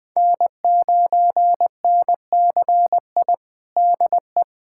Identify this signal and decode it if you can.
Morse code, heard as a keyed tone, about 19 wpm, 700 Hz. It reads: N9NCI DE
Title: USA Callsign DE Callsign - 5 Characters - 20wpm